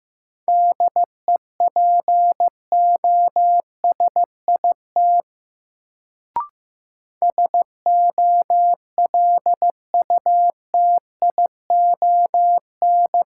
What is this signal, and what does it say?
Morse code, heard as a keyed tone, about 15 wpm, 700 Hz. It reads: DEPOSIT SOLUTION
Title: Sentences from Top 1300 Words - New Word Drill - 15wpm